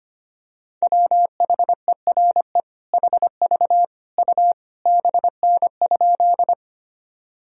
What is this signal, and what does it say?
Morse code, heard as a keyed tone, about 25 wpm, 700 Hz. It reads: WHERE HV U BN?
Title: Sentences from Top 200 Words and Abbreviations 25wpm